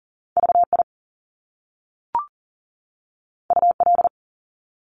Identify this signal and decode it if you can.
Morse code, heard as a keyed tone, about 41 wpm, 700 Hz. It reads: VI UL
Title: Top Two Letter Combinations in Words 40wpm